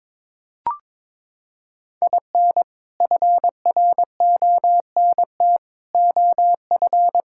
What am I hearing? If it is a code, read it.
Morse code, heard as a keyed tone, about 21 wpm, 700 Hz. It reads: IN FRONT OF
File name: practice / 22wpm / Sentences with 4-Word Phrases - 22wpm